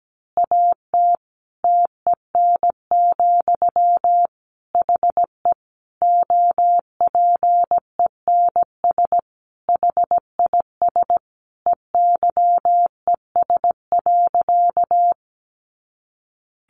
Morse code, heard as a keyed tone, about 17 wpm, 700 Hz. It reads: AT TEN, HE OPENS HIS EYES.
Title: Easy Phrases 17wpm